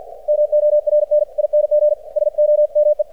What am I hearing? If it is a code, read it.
Morse code, heard as a keyed tone, about 51 wpm, 600 Hz. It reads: MORNING SOME
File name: archived / CRSnet